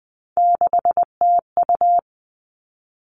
Morse code, heard as a keyed tone, about 20 wpm, 700 Hz. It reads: 6TU
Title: All Cut Numbers - 3-Digit 20wpm